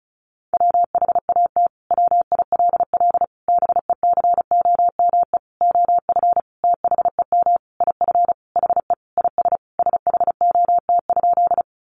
Morse code, heard as a keyed tone, about 34 wpm, 700 Hz. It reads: WHAT WILL BECOME OF THEM IF HE IS SHOT?
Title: Sentences from Top 1100 Words 35wpm